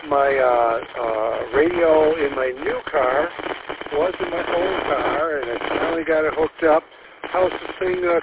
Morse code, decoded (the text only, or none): none